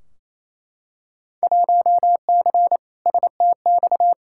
Morse code, heard as a keyed tone, about 28 wpm, 700 Hz. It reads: 1C STX
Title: Field Day Exchanges Encore - 28wpm